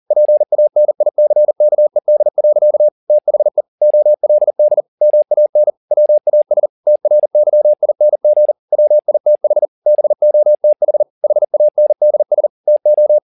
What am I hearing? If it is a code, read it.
Morse code, heard as a keyed tone, about 40 wpm, 600 Hz. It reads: PANICKED. THE OLD MAN WAS TRYING WITH BOTH HANDS TO